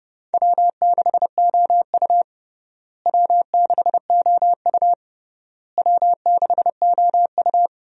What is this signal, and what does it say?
Morse code, heard as a keyed tone, about 30 wpm, 700 Hz. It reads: W6OU W6OU W6OU
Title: USA Call Signs - 4 Characters - Repeated 3x - 30wpm